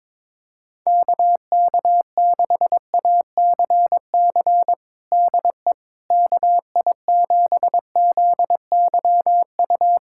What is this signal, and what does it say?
Morse code, heard as a keyed tone, about 22 wpm, 700 Hz. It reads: KK6ACC DE KI7ZYU
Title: USA Callsign DE Callsign - 6 Characters - 22wpm